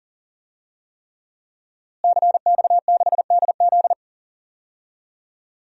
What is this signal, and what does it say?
Morse code, heard as a keyed tone, about 39 wpm, 700 Hz. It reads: CX6DZ
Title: DXCC-100 Call Signs - 5 Characters - Repeated 3x - 40wpm